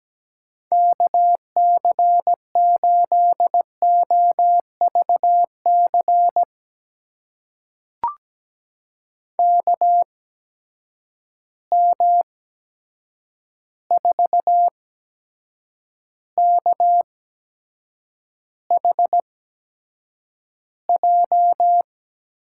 Morse code, heard as a keyed tone, about 17 wpm, spaced as for 7 wpm, 700 Hz. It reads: KC8OVC KM4KHJ
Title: USA Call Signs - 6 Characters - 8x Character Spacing - 17wpm